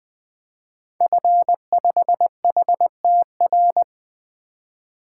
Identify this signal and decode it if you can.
Morse code, heard as a keyed tone, about 20 wpm, 700 Hz. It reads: F5HTR
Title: 2023 SOTA Activator Call Signs - 20wpm